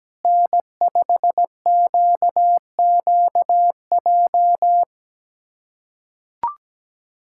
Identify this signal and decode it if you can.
Morse code, heard as a keyed tone, about 17 wpm, 700 Hz. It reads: N5QQJ
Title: USA Callsign DE Callsign - 5 Characters - 17wpm